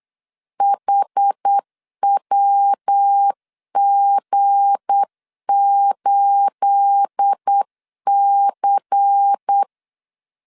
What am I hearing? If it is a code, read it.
Morse code, heard as a keyed tone, about 8 wpm, 800 Hz. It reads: HWG8C
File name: LEZIONI / 9 28 8wpm 3512